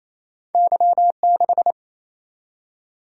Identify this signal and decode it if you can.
Morse code, heard as a keyed tone, about 28 wpm, 700 Hz. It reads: Y6
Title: Single-Letter Single-Number 28wpm